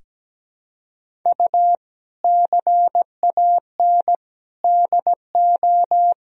Morse code, heard as a keyed tone, about 17 wpm, 700 Hz. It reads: U CAN DO
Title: Sentences from Top 100 Words and Abbreviations 17wpm